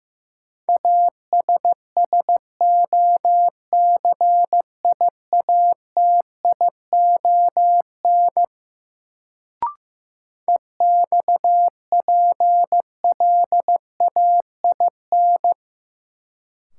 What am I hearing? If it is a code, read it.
Morse code, heard as a keyed tone, about 15 wpm, 700 Hz. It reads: ASSOCIATION EXPLAIN
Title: Sentences from Top 1100 Words - New Word Drill Alt - 15wpm